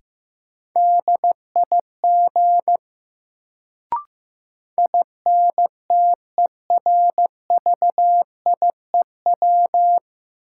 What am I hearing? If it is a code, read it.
Morse code, heard as a keyed tone, about 15 wpm, 700 Hz. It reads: DIG INTERVIEW
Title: Sentences from Top 1600 Words - New Word Drill Alt - 15wpm